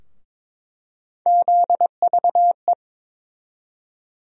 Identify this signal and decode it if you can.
Morse code, heard as a keyed tone, about 22 wpm, 700 Hz. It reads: ZVE
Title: USA Call Sign Postfixes - 3 Letters 22wpm